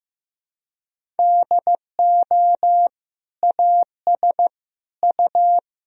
Morse code, heard as a keyed tone, about 15 wpm, 700 Hz. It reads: DO AS U
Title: Sentences from Top 100 Words and Abbreviations 15wpm